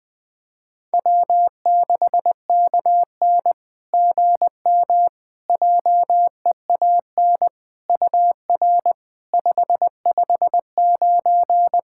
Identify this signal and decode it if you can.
Morse code, heard as a keyed tone, about 20 wpm, 690 Hz. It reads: W6KN GM JEAN UR 559
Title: SOTA QSO Practice 20wpm